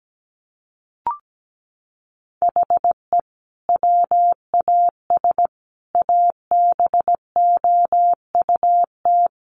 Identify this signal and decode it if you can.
Morse code, heard as a keyed tone, about 17 wpm, 700 Hz. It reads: HE WAS ABOUT
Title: Sentences from Top 1500 Words 17wpm